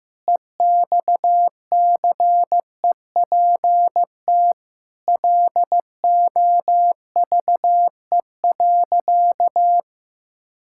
Morse code, heard as a keyed tone, about 15 wpm, 700 Hz. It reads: EXCEPT LOVE.
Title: Sentences from Top 1600 Words 15wpm